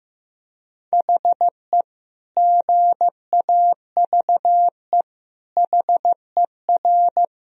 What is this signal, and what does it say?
Morse code, heard as a keyed tone, about 15 wpm, 700 Hz. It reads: HE GAVE HER
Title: Sentences from Top 1000 Words - Review Part 1 - 15wpm